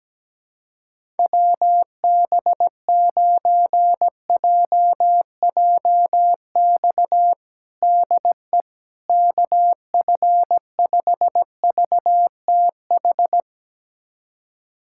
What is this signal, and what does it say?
Morse code, heard as a keyed tone, about 17 wpm, 700 Hz. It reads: WB9JJX DE KF5VTH
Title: USA Callsign DE Callsign - 6 Characters - 17wpm